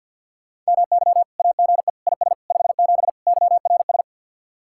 Morse code, heard as a keyed tone, about 63 wpm, 700 Hz. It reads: MY AGE IS 57 YRS